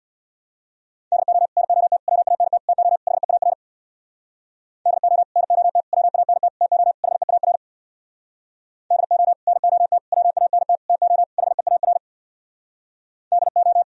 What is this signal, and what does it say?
Morse code, heard as a keyed tone, about 64 wpm, 700 Hz. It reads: DO NOT WANT TO HEAR DO NOT WANT TO HEAR DO NOT WANT TO HEAR DO